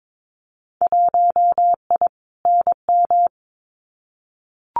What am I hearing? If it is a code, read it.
Morse code, heard as a keyed tone, about 22 wpm, 700 Hz. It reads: 1I NM
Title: Winter Field Day Exchanges Encore 22wpm